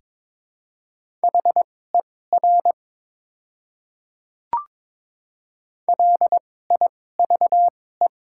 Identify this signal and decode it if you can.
Morse code, heard as a keyed tone, about 22 wpm, spaced as for 16 wpm, 700 Hz. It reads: HER LIVE
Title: 2x Character Spacing - Top 200 Words - 22wpm